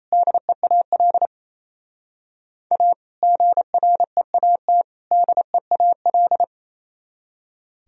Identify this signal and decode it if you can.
Morse code, heard as a keyed tone, about 36 wpm, 700 Hz. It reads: DEAL A GREAT DEAL